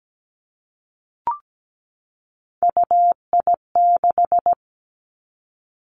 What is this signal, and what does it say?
Morse code, heard as a keyed tone, about 17 wpm, 700 Hz. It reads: UI6
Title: Three Challenging Letter-Number Combinations 17wpm